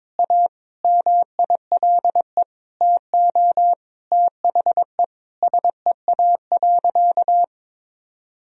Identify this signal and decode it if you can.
Morse code, heard as a keyed tone, about 22 wpm, 700 Hz. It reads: A MILE TO THE SEA.
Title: Sentences from Top 1000 Words - Review Part 1 - 22wpm